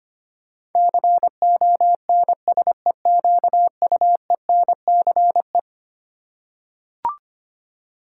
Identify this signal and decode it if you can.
Morse code, heard as a keyed tone, about 25 wpm, 700 Hz. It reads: CONSEQUENCE
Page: morse-code-ninja-practice-25wpm